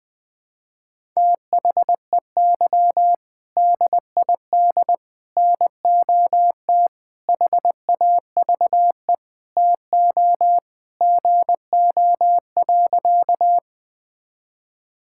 Morse code, heard as a keyed tone, about 20 wpm, 700 Hz. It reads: THEY DID NOT HAVE TO GO.
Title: Sentences with 4-Word Phrases - 20wpm